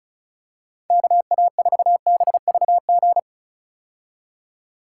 Morse code, heard as a keyed tone, about 34 wpm, 700 Hz. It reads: KA4BVG
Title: K1USN SST QSO Practice - 35wpm